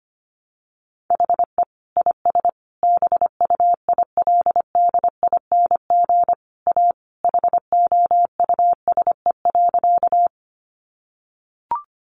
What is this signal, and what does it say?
Morse code, heard as a keyed tone, about 25 wpm, 700 Hz. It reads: HE IS BUILDING A HOUSE.
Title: Sentences from Top 700 Words 25wpm